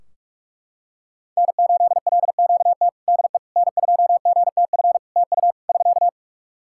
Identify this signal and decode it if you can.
Morse code, heard as a keyed tone, about 44 wpm, 700 Hz. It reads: N8LQT DE N2GTF TU <SK>
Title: K1USN SST QSO Practice 45wpm